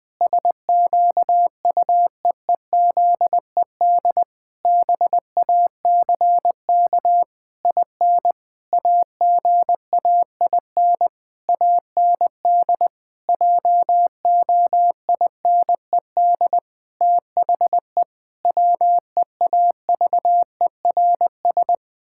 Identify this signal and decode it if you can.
Morse code, heard as a keyed tone, about 20 wpm, 700 Hz. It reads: SQUEEZED BACK IN AGAIN AND JOINED THE WEAVERS
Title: Pangram Challenge v2 - 20wpm